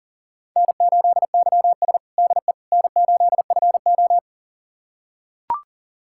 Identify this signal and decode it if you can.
Morse code, heard as a keyed tone, about 40 wpm, 690 Hz. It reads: N8YS DE N8FO
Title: USA Callsign DE Callsign - 4 Characters - 40wpm